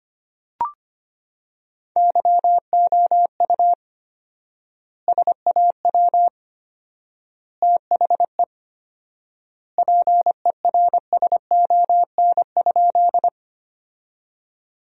Morse code, heard as a keyed tone, about 25 wpm, 700 Hz. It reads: YOU SAW THE PERSON?
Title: 4x Word-Spacing - Sentences from Top 400 Words - 25wpm